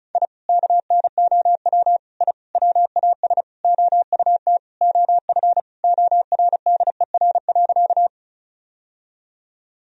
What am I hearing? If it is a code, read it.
Morse code, heard as a keyed tone, about 34 wpm, 700 Hz. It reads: I KNOW I WAS OUT OF ORDER.
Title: Sentences from Top 1000 Words - Review Part 2 - 35wpm